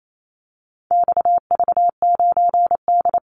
Morse code, heard as a keyed tone, about 28 wpm, 700 Hz. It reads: XV9D
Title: Call Signs Repeated 3x - 28wpm